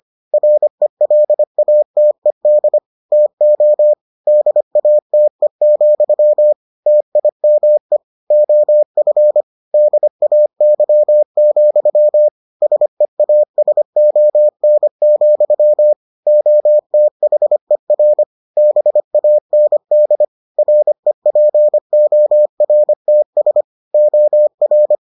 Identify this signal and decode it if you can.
Morse code, heard as a keyed tone, about 25 wpm, 600 Hz. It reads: RELATED TO DATE, TIME OF DAY, SEASON, OTHER BAND REPORTS OR